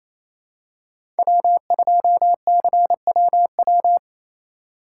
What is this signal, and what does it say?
Morse code, heard as a keyed tone, about 28 wpm, 700 Hz. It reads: W2CWW
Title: USA Call Signs - 6x Character Spacing - 28wpm